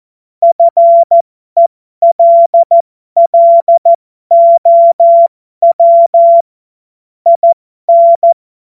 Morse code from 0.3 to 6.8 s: FELLOW